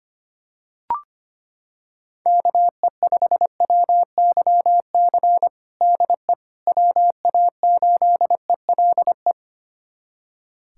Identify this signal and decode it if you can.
Morse code, heard as a keyed tone, about 25 wpm, 700 Hz. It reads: KE5WYC DE WA8ELE
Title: USA Callsign DE Callsign - 6 Characters - 25wpm